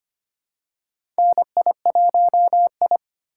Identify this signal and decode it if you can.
Morse code, heard as a keyed tone, about 25 wpm, 700 Hz. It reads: NI1I
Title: USA Call Signs - 4 Characters - 2x Character Spacing - 25wpm